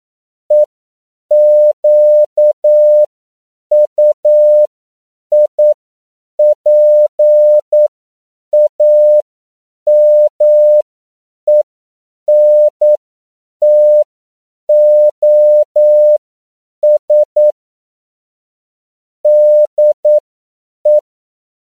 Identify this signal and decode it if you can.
Morse code, heard as a keyed tone, about 9 wpm, spaced as for 7 wpm, 600 Hz. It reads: EQUIPAMENTOS DE
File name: CW / 10PPM